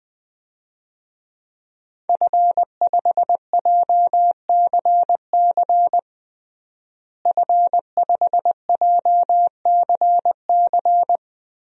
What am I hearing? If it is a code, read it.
Morse code, heard as a keyed tone, about 20 wpm, 700 Hz. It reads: F5JCC F5JCC
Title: DXCC-100 Call Signs - 5 Characters - Repeated 3x - 20wpm